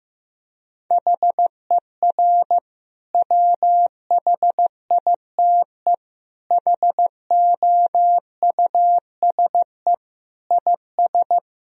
Morse code, heard as a keyed tone, about 16 wpm, 700 Hz. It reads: HER WHITE HOUSE IS
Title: Sentences from Top 1000 Words - Review Part 1 - 15wpm